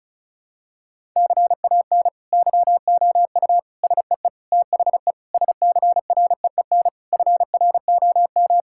Morse code, heard as a keyed tone, about 34 wpm, 700 Hz. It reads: CAN YOU SEE THE SCREEN FROM
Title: Sentences from Top 1000 Words - Review Part 6 - 35wpm